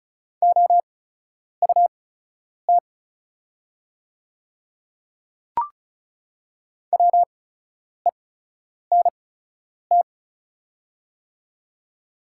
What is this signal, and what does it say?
Morse code, heard as a keyed tone, about 36 wpm, spaced as for 10 wpm, 690 Hz. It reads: OUT WENT
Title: 8x Character Spacing - Top 200 Words - 35wpm